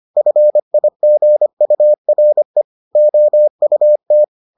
Morse code, heard as a keyed tone, about 25 wpm, 600 Hz. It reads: FIGURE OUT